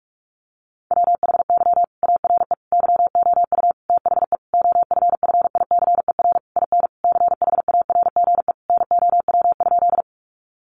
Morse code, heard as a keyed tone, about 44 wpm, 700 Hz. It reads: WHY ARE YOU THE OFFICER IN CHARGE NOW?
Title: Sentences from Top 1200 Words 45wpm